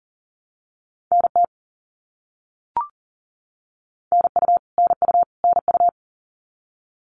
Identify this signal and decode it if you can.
Morse code, heard as a keyed tone, about 40 wpm, 700 Hz. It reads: NT NU NU NU
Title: Canadian Provinces and Territories - Repeated 3x - 40wpm